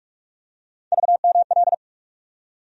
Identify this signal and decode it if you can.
Morse code, heard as a keyed tone, about 44 wpm, 700 Hz. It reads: VML
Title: USA Call Sign Postfixes - 3 Letters 45wpm